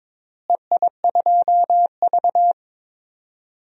Morse code, heard as a keyed tone, about 22 wpm, 700 Hz. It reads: EI2V
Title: DXCC-100 Call Signs - 4 Characters - Repeated 3x - 22wpm